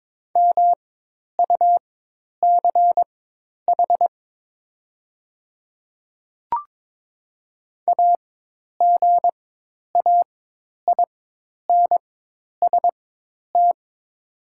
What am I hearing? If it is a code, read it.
Morse code, heard as a keyed tone, about 22 wpm, spaced as for 10 wpm, 700 Hz. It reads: MUCH AGAINST
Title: 4x Character Spacing - Top 500 Words - 22wpm